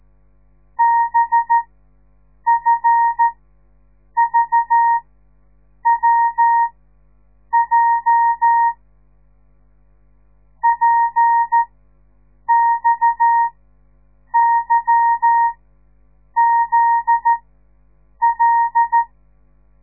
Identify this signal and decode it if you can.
Morse code, heard as a keyed tone, about 13 wpm, spaced as for 7 wpm, 930 Hz. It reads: BFVWJ PXYZL
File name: radiomorningstar / morse / les 9